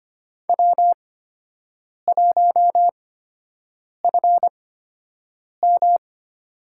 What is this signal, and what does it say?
Morse code, heard as a keyed tone, about 25 wpm, spaced as for 7 wpm, 700 Hz. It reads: W1FM
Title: USA Call Signs - 4 Characters - 8x Character Spacing - 25wpm